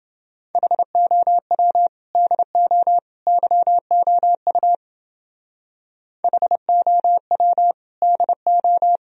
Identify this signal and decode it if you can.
Morse code, heard as a keyed tone, about 28 wpm, 700 Hz. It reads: HOW DO YOU HOW DO